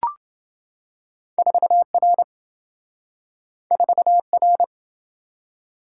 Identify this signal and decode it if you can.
Morse code, heard as a keyed tone, about 27 wpm, 700 Hz. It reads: 4R 4R